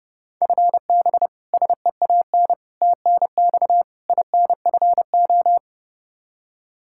Morse code, heard as a keyed tone, about 30 wpm, 700 Hz. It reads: FB SEAN TNX INFO
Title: Practice QSO - Exchange 3 - More Info - 30wpm